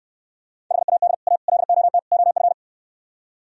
Morse code, heard as a keyed tone, about 70 wpm, 700 Hz. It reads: HAD A LOT OF